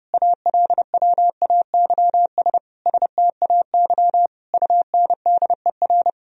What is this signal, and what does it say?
Morse code, heard as a keyed tone, about 30 wpm, 700 Hz. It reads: ALWAYS STAY UNDER
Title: Sentences from Top 1000 Words - Review Part 1 - 30wpm